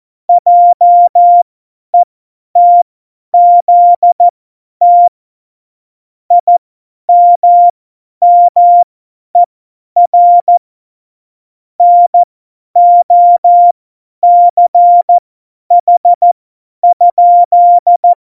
Morse code, heard as a keyed tone, about 14 wpm, spaced as for 10 wpm, 700 Hz. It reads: JETZT IMMER NOCH?
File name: Morsekurs / Witze / Witz 14. Januar 2020, 50 F70